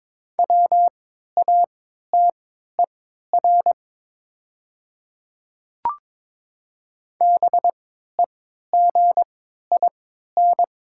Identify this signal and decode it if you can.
Morse code, heard as a keyed tone, about 22 wpm, spaced as for 12 wpm, 700 Hz. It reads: WATER BEGIN
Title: 3x Character Spacing - Top 300 Words - 22wpm